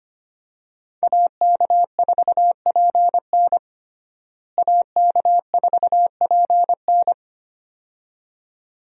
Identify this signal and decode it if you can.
Morse code, heard as a keyed tone, about 25 wpm, 700 Hz. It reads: AK4PN AK4PN
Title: USA Call Signs - 5 Characters - Repeated 2x - 25wpm